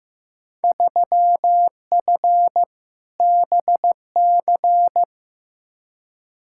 Morse code, heard as a keyed tone, about 15 wpm, 700 Hz. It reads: 3F BC
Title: Field Day Exchanges Encore - 15wpm